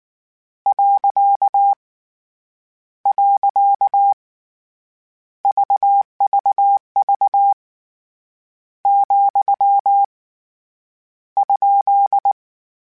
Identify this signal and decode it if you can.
Morse code, heard as a keyed tone, about 19 wpm, 800 Hz. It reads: . VVV , ?
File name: Misc / Misc-mp3s / Punct @18